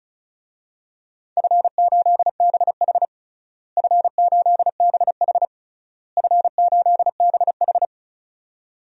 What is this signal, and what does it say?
Morse code, heard as a keyed tone, about 36 wpm, 700 Hz. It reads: F8BH F8BH F8BH
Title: DXCC-100 Call Signs - 4 Characters - Repeated 3x - 35wpm